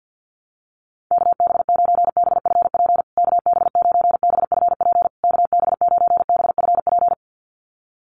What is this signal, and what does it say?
Morse code, heard as a keyed tone, about 50 wpm, 700 Hz. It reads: KB9BFP KB9BFP KB9BFP
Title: USA Call Signs Repeated 3x 50wpm